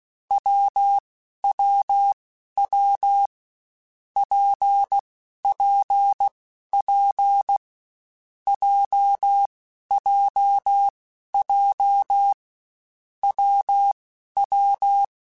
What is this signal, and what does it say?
Morse code, heard as a keyed tone, about 16 wpm, 800 Hz. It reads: W W W P P P J J J W W
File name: Difficult Combos / Difficult Combos-mp3s / WPJ WPJ1 @15